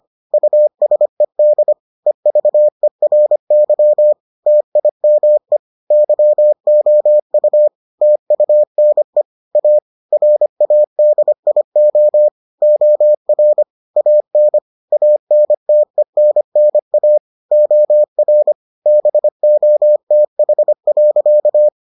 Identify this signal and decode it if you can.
Morse code, heard as a keyed tone, about 25 wpm, 600 Hz. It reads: USED EVERY TIME YOU TUNE A RADIO OR AN ANTENNA OR BOTH.